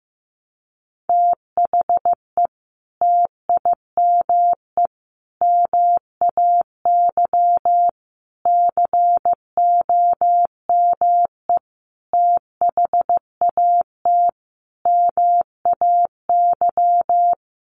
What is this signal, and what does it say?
Morse code, heard as a keyed tone, about 15 wpm, 700 Hz. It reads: THE TIME MAY COME THAT MAY
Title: Sentences from Top 1100 Words 15wpm